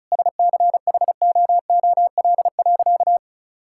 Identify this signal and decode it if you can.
Morse code, heard as a keyed tone, about 34 wpm, 700 Hz. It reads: SCHOOL.
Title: Sentences from Top 1000 Words - Review Part 5 - 35wpm